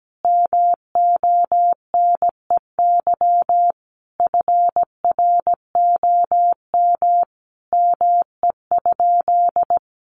Morse code, heard as a keyed tone, about 17 wpm, 700 Hz. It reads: MONEY FROM ME?
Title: Sentences from Top 1500 Words 17wpm